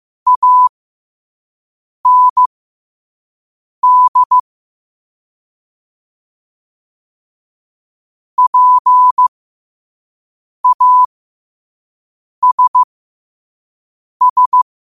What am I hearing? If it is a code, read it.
Morse code, heard as a keyed tone, about 15 wpm, spaced as for 5 wpm, 1000 Hz. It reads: AND PASS
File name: qotdfiles / 5wpm